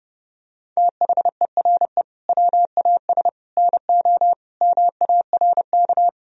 Morse code, heard as a keyed tone, about 30 wpm, 700 Hz. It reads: THERE WAS NO MARK
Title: Sentences from Top 1000 Words - Review Part 1 - 30wpm